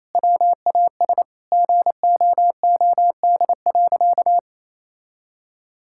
Morse code, heard as a keyed tone, about 27 wpm, 700 Hz. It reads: WAS GOOD.
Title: Sentences from Top 1000 Words - Review Part 5 - 28wpm